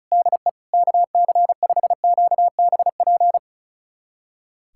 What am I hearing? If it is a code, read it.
Morse code, heard as a keyed tone, about 36 wpm, 700 Hz. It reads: DE KC5QBP
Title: USA Callsign DE Callsign - 6 Characters - 35wpm